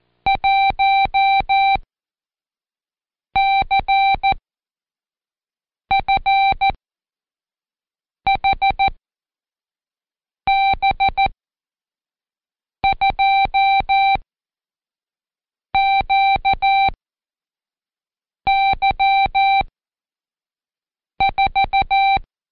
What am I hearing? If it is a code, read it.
Morse code, heard as a keyed tone, about 14 wpm, 780 Hz. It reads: 1 C F H B 2 Q Y 4
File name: Code Quick / AudioDownld1 / Seg74-Day8-Pract 1234